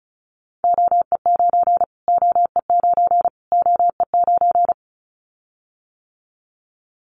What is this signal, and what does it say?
Morse code, heard as a keyed tone, about 34 wpm, 700 Hz. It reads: OE9 OE9 OE9
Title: Callsign Prefixes for Top 25 DXCC - Repeated 3x - 35wpm